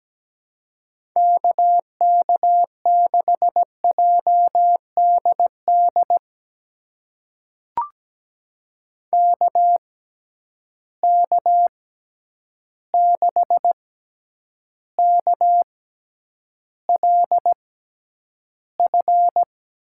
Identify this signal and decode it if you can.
Morse code, heard as a keyed tone, about 17 wpm, spaced as for 9 wpm, 700 Hz. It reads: KK6JDD KK6KLF
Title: USA Call Signs - 6 Characters - 6x Character Spacing - 17wpm